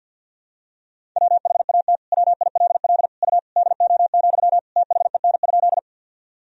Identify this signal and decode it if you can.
Morse code, heard as a keyed tone, about 51 wpm, 700 Hz. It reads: WHAT WILL U DO, THEN?
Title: Sentences from Top 100 Words and Abbreviations 50wpm